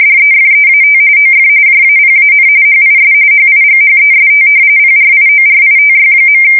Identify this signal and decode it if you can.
RTTY, about 45.45 baud. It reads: DEFEAT OF JAPAN HIS HIGHEST PRIORITY.
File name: fsk170 / usnavy4